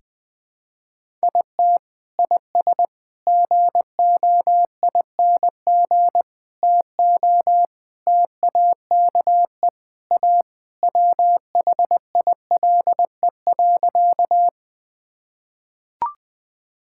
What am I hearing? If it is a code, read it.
Morse code, heard as a keyed tone, about 20 wpm, 700 Hz. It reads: IT IS GOING TO TAKE A WHILE.
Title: Sentences with 4-Word Phrases - 20wpm